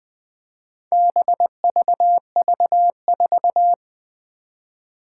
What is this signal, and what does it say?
Morse code, heard as a keyed tone, about 21 wpm, 700 Hz. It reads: BVV4
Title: Four Challenging Letter-Number Combinations 20wpm